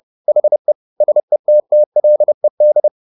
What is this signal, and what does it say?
Morse code, heard as a keyed tone, about 30 wpm, 600 Hz. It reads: HE SETTLED
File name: the old man and the sea / the old man and the sea 1 30wpm